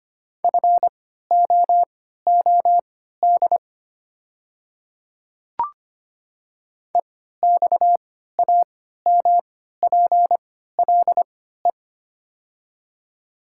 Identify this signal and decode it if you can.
Morse code, heard as a keyed tone, about 25 wpm, spaced as for 14 wpm, 700 Hz. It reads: FOOD EXAMPLE
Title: 3x Character Spacing - Top 300 Words - 25wpm